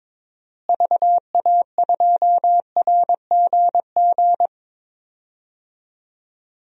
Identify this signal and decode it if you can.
Morse code, heard as a keyed tone, about 22 wpm, 700 Hz. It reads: VA2RGG
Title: DXCC-100 Call Signs - 6 Characters - Repeated 2x - 22wpm